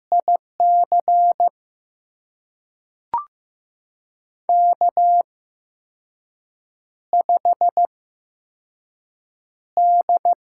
Morse code, heard as a keyed tone, about 15 wpm, spaced as for 5 wpm, 700 Hz. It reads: IC K5D
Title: USA Call Signs - 5 Characters - 8x Character Spacing - 15wpm